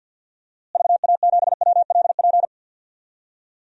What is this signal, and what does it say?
Morse code, heard as a keyed tone, about 50 wpm, 700 Hz. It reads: VA7WLP
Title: DXCC-100 Call Signs - 6 Characters - Repeated 2x - 50wpm